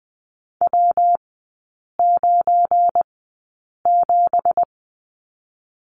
Morse code, heard as a keyed tone, about 20 wpm, spaced as for 8 wpm, 690 Hz. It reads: W97
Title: 172 Lesson 28e - Number 9 - Three Characters 20wpm